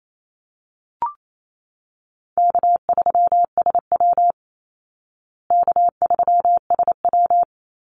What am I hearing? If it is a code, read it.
Morse code, heard as a keyed tone, about 28 wpm, 700 Hz. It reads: K3SW K3SW
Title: Easy Call Signs Repeated 3x - 28wpm